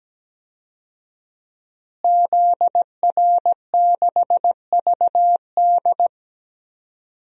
Morse code, heard as a keyed tone, about 17 wpm, 700 Hz. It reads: ZR6VD
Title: DXCC-100 Call Signs - 5 Characters - Repeated 2x - 17wpm